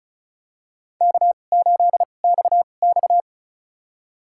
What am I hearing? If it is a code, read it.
Morse code, heard as a keyed tone, about 36 wpm, spaced as for 25 wpm, 700 Hz. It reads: K8XX
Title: USA Call Signs - 4 Characters - 2x Character Spacing - 35wpm